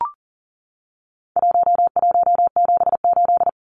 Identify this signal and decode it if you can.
Morse code, heard as a keyed tone, about 40 wpm, 700 Hz. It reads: 1178
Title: Numbers - 4-Digit 40wpm